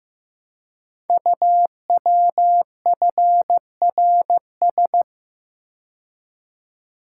Morse code, heard as a keyed tone, about 15 wpm, 700 Hz. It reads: UWFRS
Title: Code Groups - 5 Letters - 15wpm